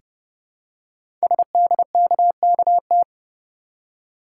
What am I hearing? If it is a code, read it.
Morse code, heard as a keyed tone, about 30 wpm, 700 Hz. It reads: SDKKT
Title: Code Groups - 5 Letters - 30wpm